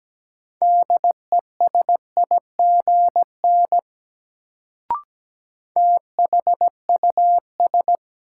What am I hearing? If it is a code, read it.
Morse code, heard as a keyed tone, about 17 wpm, 700 Hz. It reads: DESIGN THUS
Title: Sentences from Top 800 Words - New Word Drill - 17wpm